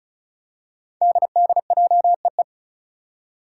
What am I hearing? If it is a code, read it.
Morse code, heard as a keyed tone, about 36 wpm, 700 Hz. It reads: DDJEE
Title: Five Challenging Letter-Number Combinations 35wpm